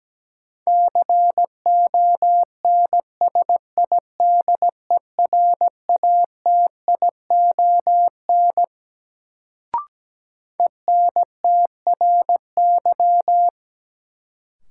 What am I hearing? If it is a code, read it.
Morse code, heard as a keyed tone, about 17 wpm, 700 Hz. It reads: CONSIDERATION ENTRY
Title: Sentences from Top 1500 Words - New Word Drill - 17wpm